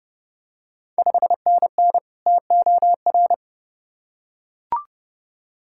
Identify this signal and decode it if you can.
Morse code, heard as a keyed tone, about 30 wpm, 690 Hz. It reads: 5NN TOR
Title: Ontario QSO Party Exchange 30wpm